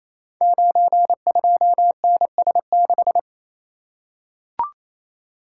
Morse code, heard as a keyed tone, about 27 wpm, 700 Hz. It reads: 92NS6
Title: Code Groups - 5 Letter:Numbers - 28wpm